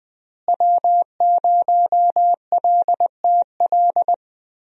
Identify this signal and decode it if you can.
Morse code, heard as a keyed tone, about 21 wpm, 700 Hz. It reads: W0LTL
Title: USA Call Signs - 5 Characters - Repeated 3x - 20wpm